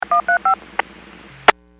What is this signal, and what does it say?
DTMF, heard as telephone touch tones, tones of about 110 ms. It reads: 135
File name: WPS / 01616 19-52 WPS 451.25 WPS